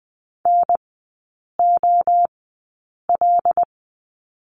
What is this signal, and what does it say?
Morse code, heard as a keyed tone, about 20 wpm, spaced as for 8 wpm, 700 Hz. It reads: NOL
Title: 093 Lesson 16e - Letter c - Three Characters 20wpm